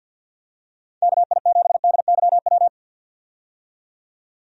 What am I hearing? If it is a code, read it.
Morse code, heard as a keyed tone, about 47 wpm, 700 Hz. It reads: KI7DYW
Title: USA Call Signs - 6 Characters - 50wpm